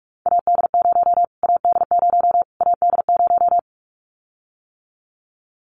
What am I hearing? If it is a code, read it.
Morse code, heard as a keyed tone, about 44 wpm, 700 Hz. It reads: AD0 AD0 AD0
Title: Callsign Prefixes for Top 25 DXCC - Repeated 3x - 45wpm